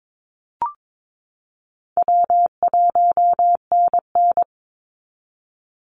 Morse code, heard as a keyed tone, about 22 wpm, 700 Hz. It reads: W1NN
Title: Call Signs Repeated 3x - 22wpm